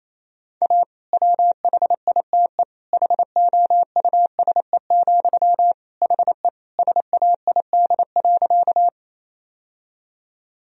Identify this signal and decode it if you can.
Morse code, heard as a keyed tone, about 28 wpm, 700 Hz. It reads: A WHITE HOUSE, HE SAID.
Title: Sentences from Top 1000 Words - Review Part 1 - 28wpm